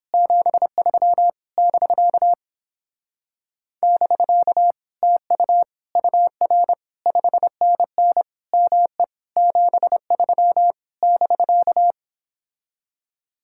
Morse code, heard as a keyed tone, about 31 wpm, 700 Hz. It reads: 73 <BK> <BK> TU UR 5NN ME 73 <BK>